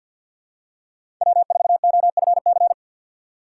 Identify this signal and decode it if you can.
Morse code, heard as a keyed tone, about 51 wpm, 690 Hz. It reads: W4OFC
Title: USA Call Signs - 5 Characters - 4x Character Spacing - 50wpm